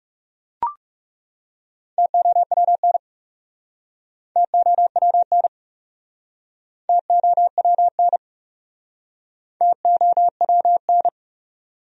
Morse code, heard as a keyed tone, about 40 wpm, 700 Hz. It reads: TOWN TOWN TOWN TOWN